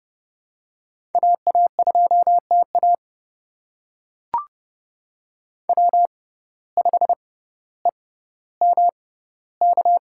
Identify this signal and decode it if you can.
Morse code, heard as a keyed tone, about 30 wpm, spaced as for 15 wpm, 700 Hz. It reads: AA2TA W5EMK
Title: USA Call Signs - 5 Characters - 6x Character Spacing - 30wpm